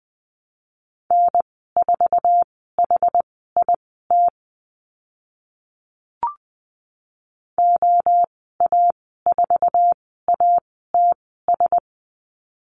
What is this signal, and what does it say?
Morse code, heard as a keyed tone, about 20 wpm, spaced as for 14 wpm, 700 Hz. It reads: N4HIT OA4ATS
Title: 069 Lesson 12g - Letter d - Call Signs 20wpm